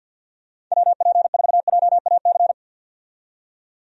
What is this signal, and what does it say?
Morse code, heard as a keyed tone, about 52 wpm, 700 Hz. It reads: WP4JAC
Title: USA Call Signs - 6 Characters - 50wpm